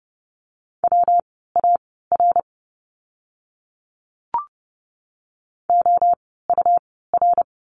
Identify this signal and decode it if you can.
Morse code, heard as a keyed tone, about 30 wpm, spaced as for 17 wpm, 700 Hz. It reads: WAR OUR
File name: course / 30:10wpm / 115 Lesson 19f - Letter w - Words 30wpm